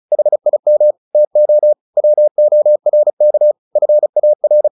Morse code, heard as a keyed tone, about 34 wpm, 600 Hz. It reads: HIM TO WORK FAR
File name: the old man and the sea / the old man and the sea 1 35wpm